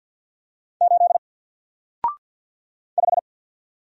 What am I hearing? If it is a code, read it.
Morse code, heard as a keyed tone, about 48 wpm, 700 Hz. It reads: 8 5
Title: Single Letter-Number Alt 50wpm